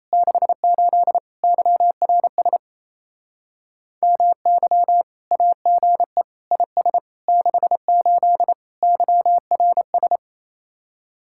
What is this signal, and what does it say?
Morse code, heard as a keyed tone, about 35 wpm, 700 Hz. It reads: 68 YRS MY AGE IS 68 YRS